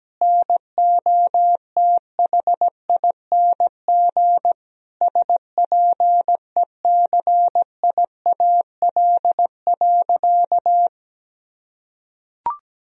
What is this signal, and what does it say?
Morse code, heard as a keyed tone, about 17 wpm, 700 Hz. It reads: NOTHING SPECIAL.
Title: Sentences from Top 1000 Words - Review Part 3 - 17wpm